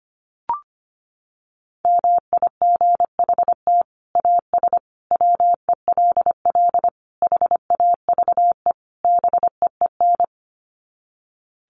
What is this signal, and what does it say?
Morse code, heard as a keyed tone, about 25 wpm, 700 Hz. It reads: MIGHT AS WELL HAVE BEEN
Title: Sets of 5 Words Encore2 25wpm